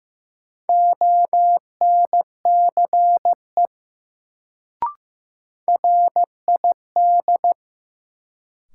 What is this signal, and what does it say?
Morse code, heard as a keyed tone, about 15 wpm, 700 Hz. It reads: ONCE RID
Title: Sets of 4 Words - New Word Drill - 15wpm